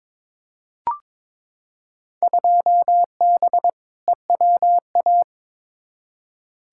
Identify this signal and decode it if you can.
Morse code, heard as a keyed tone, about 22 wpm, 700 Hz. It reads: 2B EWA
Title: Field Day Exchanges - 22wpm